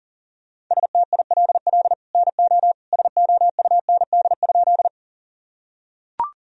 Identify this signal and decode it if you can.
Morse code, heard as a keyed tone, about 40 wpm, 700 Hz. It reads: STILL NO SOUND?
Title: Sentences from Top 1000 Words - Review Part 1 - 40wpm